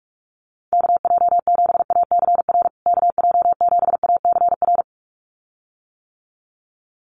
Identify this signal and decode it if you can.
Morse code, heard as a keyed tone, about 44 wpm, 700 Hz. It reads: KJ7ACR KJ7ACR
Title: USA Call Signs Repeated 2x 45wpm